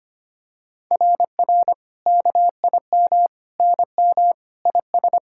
Morse code, heard as a keyed tone, about 25 wpm, 700 Hz. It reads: RR KIM NM IS
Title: SOTA QSO Practice 25wpm